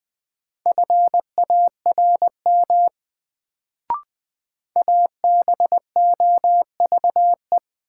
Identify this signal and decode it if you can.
Morse code, heard as a keyed tone, about 20 wpm, 700 Hz. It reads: FARM ABOVE
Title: Sentences from Top 1000 Words - Review Part 2 - New Word Drill - 20wpm